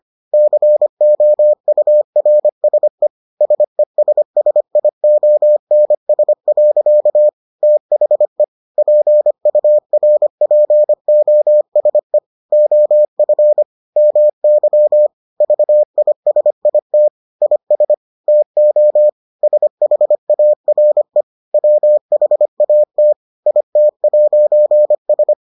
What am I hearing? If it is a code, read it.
Morse code, heard as a keyed tone, about 25 wpm, 600 Hz. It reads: COURSE SESSIONS. THE PURPOSE OF MY VISIT IS TO SHARE WHAT IT'S